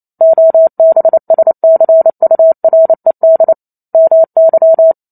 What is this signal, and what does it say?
Morse code, heard as a keyed tone, about 28 wpm, 650 Hz. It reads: OBSCURED MY